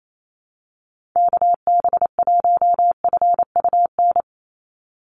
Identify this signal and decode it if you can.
Morse code, heard as a keyed tone, about 28 wpm, 700 Hz. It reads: KB1FUN
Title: USA Call Signs 28wpm